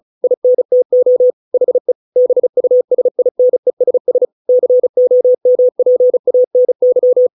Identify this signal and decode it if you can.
Morse code, heard as a keyed tone, about 34 wpm, 500 Hz. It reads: INTO HE BUSINESS COMPANY
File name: 632 words / 632 Words 35WPM